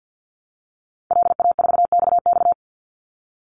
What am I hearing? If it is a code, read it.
Morse code, heard as a keyed tone, about 50 wpm, 700 Hz. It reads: LA4XX
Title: Call Signs Repeated 2x - 50wpm